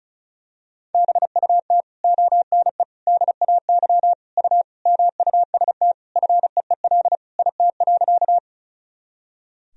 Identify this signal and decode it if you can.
Morse code, heard as a keyed tone, about 36 wpm, 700 Hz. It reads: BUT ONE DAY U MUST FEEL IT.
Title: Sentences from Top 400 Words and Abbreviations 35wpm